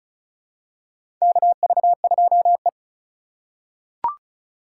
Morse code, heard as a keyed tone, about 34 wpm, 700 Hz. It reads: KV2E
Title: USA Call Signs - 4 Characters - 3x Character Spacing - 35wpm